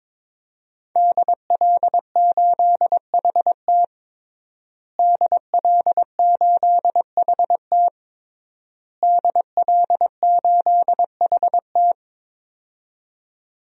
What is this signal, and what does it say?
Morse code, heard as a keyed tone, about 21 wpm, 700 Hz. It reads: DL8HT DL8HT DL8HT
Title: DXCC-100 Call Signs - 5 Characters - Repeated 3x - 22wpm